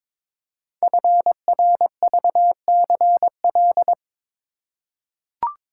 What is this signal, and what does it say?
Morse code, heard as a keyed tone, about 22 wpm, 700 Hz. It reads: FRVCL
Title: Code Groups - 5 Letters - 22wpm